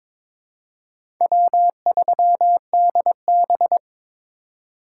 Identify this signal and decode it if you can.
Morse code, heard as a keyed tone, about 22 wpm, 700 Hz. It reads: W3DB
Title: USA Call Signs - 4x Character Spacing - 22wpm